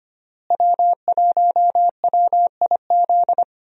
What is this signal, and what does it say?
Morse code, heard as a keyed tone, about 25 wpm, 700 Hz. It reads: W1WIZ
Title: USA Call Signs - 5 Characters - Repeated 3x - 25wpm